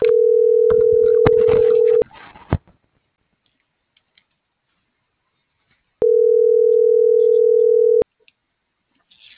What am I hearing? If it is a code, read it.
Ringback tone (North America), 440 + 480 Hz.